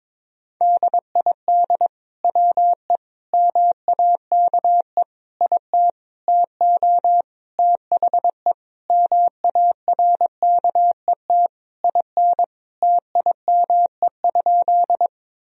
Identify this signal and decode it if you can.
Morse code, heard as a keyed tone, about 21 wpm, 700 Hz. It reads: DID WE MAKE IT TO THE MARKET IN TIME?
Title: Sentences from Top 1000 Words - Review Part 3 - 22wpm